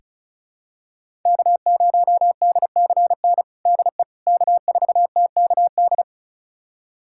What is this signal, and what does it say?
Morse code, heard as a keyed tone, about 34 wpm, 700 Hz. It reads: K0DCN DE K4TKD
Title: USA Callsign DE Callsign - 5 Characters - 35wpm